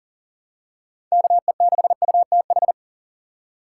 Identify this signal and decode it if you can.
Morse code, heard as a keyed tone, about 40 wpm, 700 Hz. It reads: KE6UTH
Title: USA Call Signs - 6 Characters - 2x Character Spacing - 40wpm